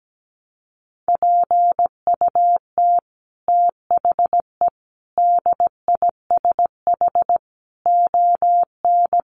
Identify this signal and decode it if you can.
Morse code, heard as a keyed tone, about 17 wpm, 700 Hz. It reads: PUT THE DISH ON
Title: Sentences from Top 1300 Words 17wpm